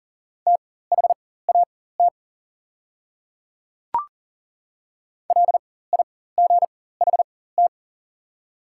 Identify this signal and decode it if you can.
Morse code, heard as a keyed tone, about 40 wpm, spaced as for 19 wpm, 700 Hz. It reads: THAT LIGHT
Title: 4x Character Spacing - Top 300 Words - 40wpm